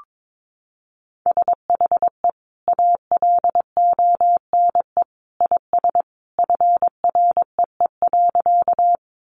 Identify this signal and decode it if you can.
Morse code, heard as a keyed tone, about 22 wpm, 700 Hz. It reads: SHE ALONE IS FREE.
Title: Sentences from Top 1000 Words 22wpm